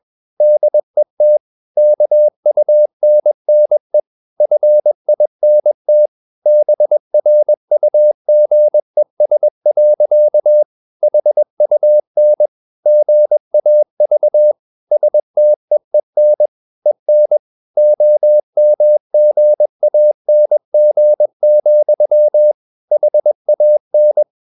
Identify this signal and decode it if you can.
Morse code, heard as a keyed tone, about 21 wpm, 600 Hz. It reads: DET KUNNE FINT BRUGES. HUN GAV STEEN EN OMGANG, HAN
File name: Ferie QRS 21wpm